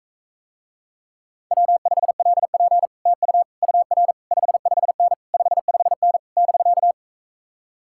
Morse code, heard as a keyed tone, about 45 wpm, 700 Hz. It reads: W5LP TU UR 55N 55N <BK>